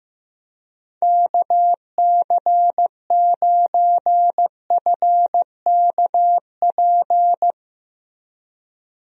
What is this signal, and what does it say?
Morse code, heard as a keyed tone, about 16 wpm, 700 Hz. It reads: KC9FKP
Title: USA Call Signs - 6 Characters - Repeated 2x - 15wpm